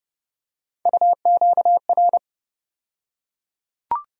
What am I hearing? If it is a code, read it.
Morse code, heard as a keyed tone, about 30 wpm, 700 Hz. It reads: UQR
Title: USA Call Sign Postfixes - 3 Letters 30wpm